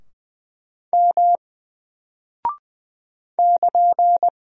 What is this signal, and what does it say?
Morse code, heard as a keyed tone, about 20 wpm, 700 Hz. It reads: M <KN>
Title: Single Character (Letter:Number:Punctuation:Prosign) 20wpm